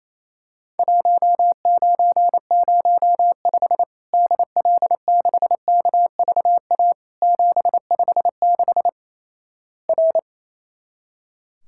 Morse code, heard as a keyed tone, about 28 wpm, 700 Hz. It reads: 1905 DL6KVA 756 R
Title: Worked All Europe DX Contest - QTC Exchange - 28wpm